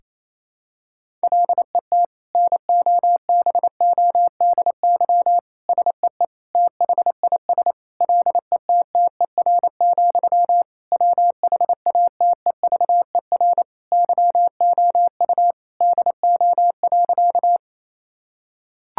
Morse code, heard as a keyed tone, about 28 wpm, 700 Hz. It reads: LET NOBODY SEE THIS LETTER, WHATEVER YOU DO.
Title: Sentences from Top 1600 Words 28wpm